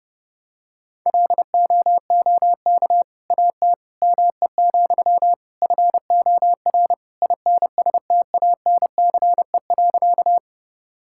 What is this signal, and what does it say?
Morse code, heard as a keyed tone, about 32 wpm, 700 Hz. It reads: LOOK AT ME, FOR INSTANCE.
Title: Sentences from Top 1000 Words - Review Part 5 - 30wpm